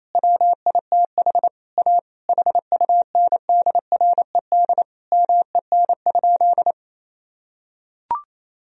Morse code, heard as a keyed tone, about 28 wpm, 700 Hz. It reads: WITH A HUNDRED MEN?
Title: Sentences from Top 1000 Words - Review Part 2 - 28wpm